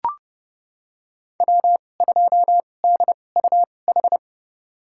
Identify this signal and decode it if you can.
Morse code, heard as a keyed tone, about 30 wpm, spaced as for 22 wpm, 700 Hz. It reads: W2DUH
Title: USA Call Signs - 2x Character Spacing - 30wpm